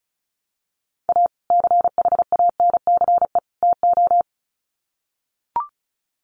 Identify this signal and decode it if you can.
Morse code, heard as a keyed tone, about 34 wpm, 700 Hz. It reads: A CHANCE TO
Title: Sets of 3 Words 35wpm